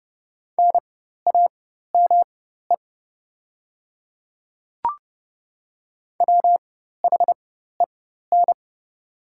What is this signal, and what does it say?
Morse code, heard as a keyed tone, about 30 wpm, spaced as for 14 wpm, 700 Hz. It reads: NAME WHEN
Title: 4x Character Spacing - Top 300 Words - 30wpm